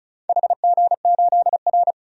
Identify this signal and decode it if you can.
Morse code, heard as a keyed tone, about 36 wpm, 700 Hz. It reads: HG8R
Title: DXCC-100 Call Signs - 4 Characters - 35wpm